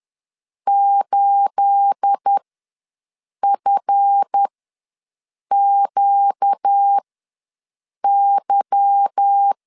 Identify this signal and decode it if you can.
Morse code, heard as a keyed tone, about 10 wpm, spaced as for 6 wpm, 800 Hz. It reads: 8FQY
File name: LEZIONI / 9 28 ws 10wpm 4124